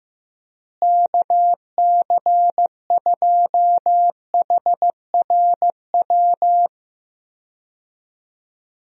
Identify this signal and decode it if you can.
Morse code, heard as a keyed tone, about 15 wpm, 700 Hz. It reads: KC2HRW
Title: USA Call Signs - 6 Characters - Repeated 3x - 15wpm